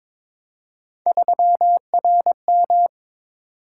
Morse code, heard as a keyed tone, about 21 wpm, 700 Hz. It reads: 3RM
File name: practice / 22wpm / Three Letter-Number Combinations - 22wpm